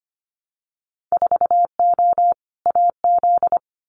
Morse code, heard as a keyed tone, about 25 wpm, 700 Hz. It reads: 4O AZ
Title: Winter Field Day Exchanges Encore 25wpm